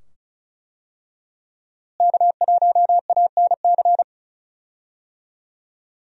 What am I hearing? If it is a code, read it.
Morse code, heard as a keyed tone, about 36 wpm, 700 Hz. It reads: K1ANC
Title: USA Call Signs - 5 Characters - Repeated 2x - 35wpm